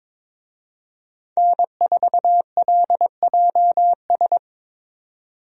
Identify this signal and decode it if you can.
Morse code, heard as a keyed tone, about 22 wpm, 700 Hz. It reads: N4LJS
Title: USA Call Signs - 5 Characters - Repeated 3x - 22wpm